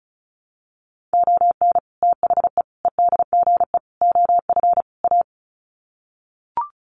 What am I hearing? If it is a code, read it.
Morse code, heard as a keyed tone, about 34 wpm, 700 Hz. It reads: ON THE EDGE OF A
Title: Sets of 5 Words Encore2 35wpm